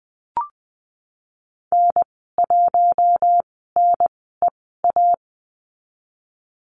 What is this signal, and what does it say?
Morse code, heard as a keyed tone, about 20 wpm, spaced as for 15 wpm, 700 Hz. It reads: N1NEA
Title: 055 Lesson 10g - Letter r - Call Signs 20wpm